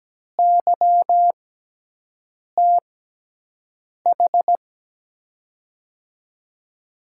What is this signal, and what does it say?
Morse code, heard as a keyed tone, about 17 wpm, spaced as for 6 wpm, 700 Hz. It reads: YTH
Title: USA Call Signs - 5 Characters - 6x Character Spacing - 17wpm